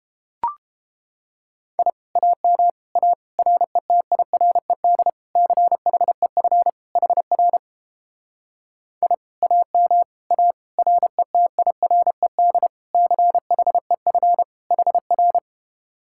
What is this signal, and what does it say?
Morse code, heard as a keyed tone, about 32 wpm, 700 Hz. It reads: I AM A RETIRED CHEF HR I AM A RETIRED CHEF HR